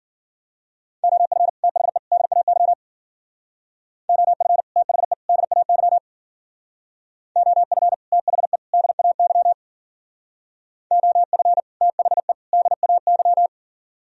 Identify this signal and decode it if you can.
Morse code, heard as a keyed tone, about 51 wpm, 700 Hz. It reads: OF THE DAY OF THE DAY OF THE DAY OF THE DAY